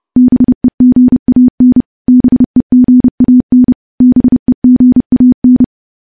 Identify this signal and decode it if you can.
Morse code, heard as a keyed tone, about 32 wpm, 250 Hz. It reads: BEGAN BEGAN BEGAN
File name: morse / morse-training-30WPM